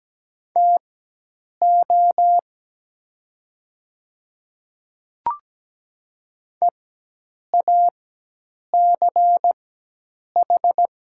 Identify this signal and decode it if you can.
Morse code, heard as a keyed tone, about 17 wpm, spaced as for 8 wpm, 700 Hz. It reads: TO EACH
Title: 4x Character Spacing - Top 100 Words - 17wpm